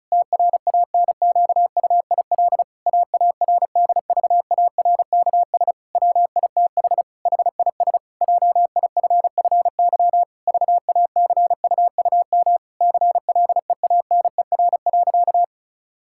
Morse code, heard as a keyed tone, about 34 wpm, 700 Hz. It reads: TRANQUIL AARDVARKS WITH HIS JIFFY VACUUM CLEANER.
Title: Pangram Challenge v2 - 35wpm